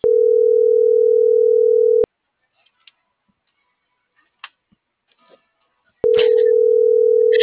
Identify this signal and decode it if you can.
Ringback tone (North America), 440 + 480 Hz.